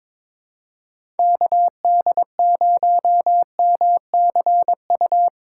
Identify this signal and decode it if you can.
Morse code, heard as a keyed tone, about 22 wpm, 700 Hz. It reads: KD0MCU
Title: USA Call Signs - 6 Characters - 2x Character Spacing - 22wpm